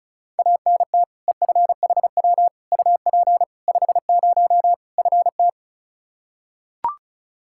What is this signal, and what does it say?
Morse code, heard as a keyed tone, about 34 wpm, 700 Hz. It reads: ANT EFHW UP 50 FT
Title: K1USN SST QSO Practice 35wpm